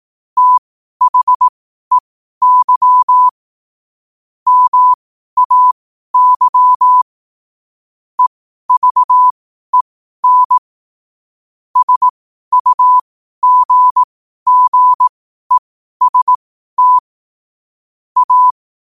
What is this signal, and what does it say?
Morse code, heard as a keyed tone, about 18 wpm, spaced as for 12 wpm, 1000 Hz. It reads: THEY MAY EVEN SUGGEST A
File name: qotdfiles / 12wpm